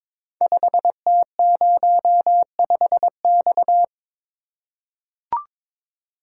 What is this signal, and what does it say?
Morse code, heard as a keyed tone, about 22 wpm, 700 Hz. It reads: 5T05X
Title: Code Groups - 5 Letter:Numbers - 22wpm